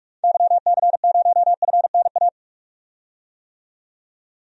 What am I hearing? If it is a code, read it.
Morse code, heard as a keyed tone, about 44 wpm, 700 Hz. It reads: YC0FNA
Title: DXCC-100 Call Signs - 6 Characters - 45wpm